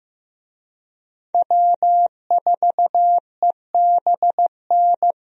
Morse code, heard as a keyed tone, about 15 wpm, 700 Hz. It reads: W4EBN
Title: USA Call Signs - 5 Characters - Repeated 2x - 15wpm